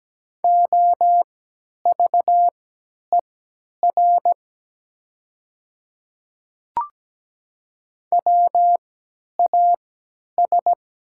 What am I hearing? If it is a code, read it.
Morse code, heard as a keyed tone, about 17 wpm, spaced as for 10 wpm, 700 Hz. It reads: OVER WAS
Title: 3x Character Spacing - Top 100 Words - 17wpm